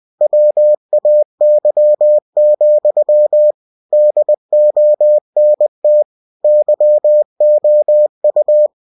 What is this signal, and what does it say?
Morse code, heard as a keyed tone, about 20 wpm, 600 Hz. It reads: WAY, DONT YOU
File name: cw-books / winnie-the-pooh / winnie-20wpm-page-0082